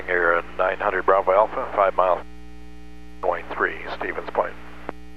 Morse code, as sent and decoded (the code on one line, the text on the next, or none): none